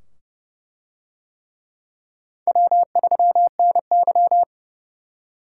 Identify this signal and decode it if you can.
Morse code, heard as a keyed tone, about 29 wpm, 700 Hz. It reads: W3NY